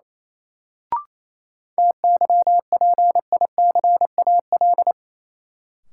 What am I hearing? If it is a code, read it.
Morse code, heard as a keyed tone, about 28 wpm, 700 Hz. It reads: TYPICAL
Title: Sentences from Top 1100 Words - New Word Drill - 28wpm